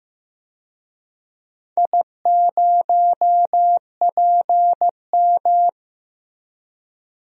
Morse code, heard as a keyed tone, about 15 wpm, 700 Hz. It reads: I0PM
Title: DXCC-100 Call Signs - 4 Characters - Repeated 3x - 15wpm